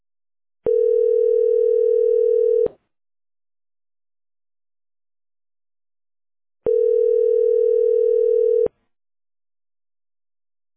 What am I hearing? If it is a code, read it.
Ringback tone (North America), 440 + 480 Hz.